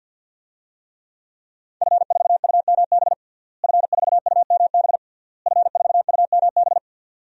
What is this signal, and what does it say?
Morse code, heard as a keyed tone, about 48 wpm, 700 Hz. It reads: F4UMB F4UMB F4UMB
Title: DXCC-100 Call Signs - 5 Characters - Repeated 3x - 50wpm